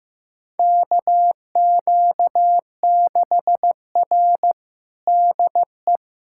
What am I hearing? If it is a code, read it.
Morse code, heard as a keyed tone, about 15 wpm, 700 Hz. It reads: KQ6R DE
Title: USA Callsign DE Callsign - 4 Characters - 15wpm